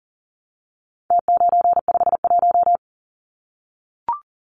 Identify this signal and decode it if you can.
Morse code, heard as a keyed tone, about 40 wpm, 700 Hz. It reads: T951
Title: Numbers - Contest Style - 4-Digit 40wpm